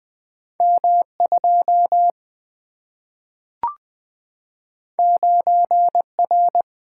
Morse code, heard as a keyed tone, about 20 wpm, 700 Hz. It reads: M2 9R
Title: Single-Letter Single-Number 20wpm